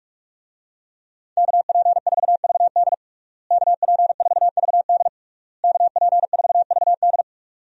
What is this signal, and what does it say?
Morse code, heard as a keyed tone, about 46 wpm, 700 Hz. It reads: KP4VD KP4VD KP4VD
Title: DXCC-100 Call Signs - 5 Characters - Repeated 3x - 45wpm